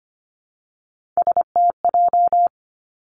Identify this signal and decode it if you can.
Morse code, heard as a keyed tone, about 25 wpm, 700 Hz. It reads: STJ
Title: Three Challenging Letter-Number Combinations 25wpm